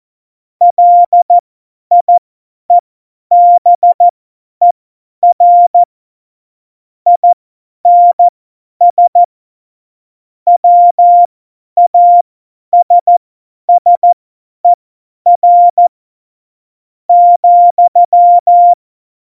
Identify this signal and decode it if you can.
Morse code, heard as a keyed tone, about 14 wpm, spaced as for 9 wpm, 700 Hz. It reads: LIEBER INS WASSER,
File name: Morsekurs / Witze / Witz 14. August 2018, 50 F70